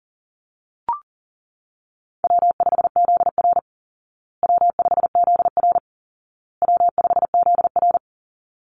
Morse code, heard as a keyed tone, about 40 wpm, 700 Hz. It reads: W5ZR W5ZR W5ZR
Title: Easy Call Signs Repeated 3x - 40wpm